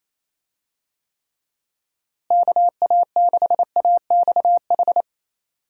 Morse code, heard as a keyed tone, about 28 wpm, 700 Hz. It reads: KA6AXH
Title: USA Call Signs - 6 Characters - Repeated 3x - 28wpm